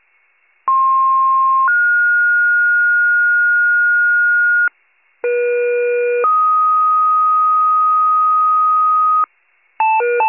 Two-tone page, heard as a one-second tone, then a three-second tone, about 1060 then 1430 Hz, about 480 then 1250 Hz, with noise. It is heard from 0.7 to 9.3 s.